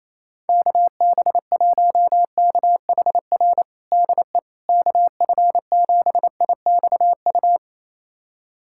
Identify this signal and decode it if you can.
Morse code, heard as a keyed tone, about 28 wpm, 700 Hz. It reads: KB1KHR DE KF7IXU
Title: USA Callsign DE Callsign - 6 Characters - 28wpm